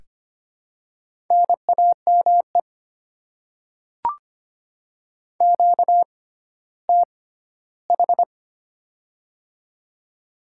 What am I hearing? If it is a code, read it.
Morse code, heard as a keyed tone, about 25 wpm, spaced as for 14 wpm, 700 Hz. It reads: NAME QTH
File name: Top 20 QSO Elements - 6x Character Spacing - 25wpm